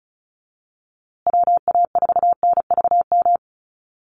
Morse code, heard as a keyed tone, about 34 wpm, 700 Hz. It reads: WA4NVM
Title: USA Call Signs Repeated 2x 35wpm